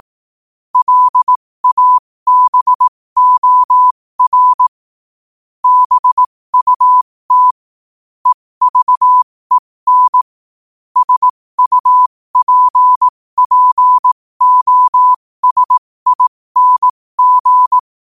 Morse code, heard as a keyed tone, about 18 wpm, spaced as for 15 wpm, 1000 Hz. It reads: LABOR BUT EVEN SUPPOSING
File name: qotdfiles / 15wpm